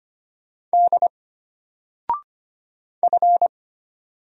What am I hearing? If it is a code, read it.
Morse code, heard as a keyed tone, about 25 wpm, 700 Hz. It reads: D F
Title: Single Letter-Number Alt 25wpm